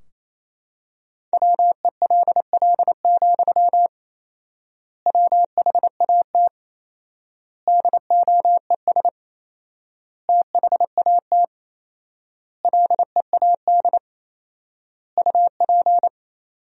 Morse code, heard as a keyed tone, about 28 wpm, 700 Hz. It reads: WELL, WHAT DOES THAT LEAD UP
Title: 4x Word-Spacing - Sentences from Top 500 Words - 28wpm